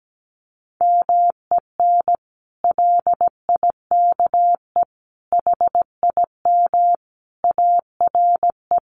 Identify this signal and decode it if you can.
Morse code, heard as a keyed tone, about 17 wpm, 700 Hz. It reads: MEN LIKE HIM ARE
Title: Sentences from Top 1100 Words 17wpm